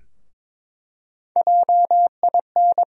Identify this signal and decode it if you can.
Morse code, heard as a keyed tone, about 22 wpm, 700 Hz. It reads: JIN
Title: USA Call Sign Postfixes - 3 Letters 22wpm